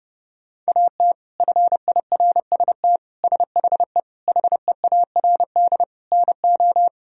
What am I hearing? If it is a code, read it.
Morse code, heard as a keyed tone, about 30 wpm, 700 Hz. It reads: AT FIRST SHE HEARD NO
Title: Sentences from Top 1000 Words - Review Part 1 - 30wpm